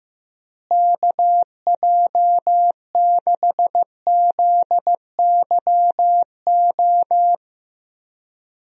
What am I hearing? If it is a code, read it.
Morse code, heard as a keyed tone, about 15 wpm, 700 Hz. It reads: KJ6ZYO
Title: USA Call Signs - 6 Characters - Repeated 3x - 15wpm